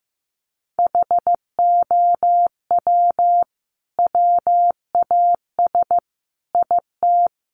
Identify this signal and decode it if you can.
Morse code, heard as a keyed tone, about 15 wpm, 700 Hz. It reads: HOW WAS IT
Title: Sentences from Top 500 Words 15wpm